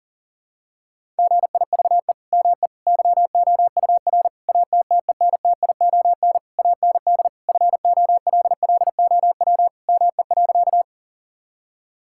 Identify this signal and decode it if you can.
Morse code, heard as a keyed tone, about 38 wpm, 700 Hz. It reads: GIVE ME YOUR ATTENTION AND FOLLOW ME.
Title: Sentences from Top 1000 Words - Review Part 6 - 40wpm